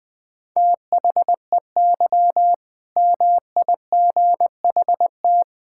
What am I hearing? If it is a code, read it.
Morse code, heard as a keyed tone, about 20 wpm, 700 Hz. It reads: THEY MIGHT
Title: All-Age Appropriate Jokes - 20wpm